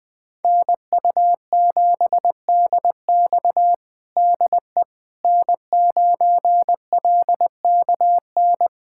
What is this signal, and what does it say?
Morse code, heard as a keyed tone, about 21 wpm, 700 Hz. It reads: NU7DX DE N9LKN
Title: USA Callsign DE Callsign - 5 Characters - 20wpm